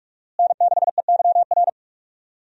Morse code, heard as a keyed tone, about 46 wpm, 700 Hz. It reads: N6EYR
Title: USA Call Signs - 5 Characters - Repeated 3x - 45wpm